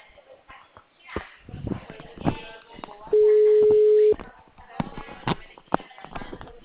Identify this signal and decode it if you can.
Ringback tone (Europe, 425 Hz), with speech.
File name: out- -2050- - - .20356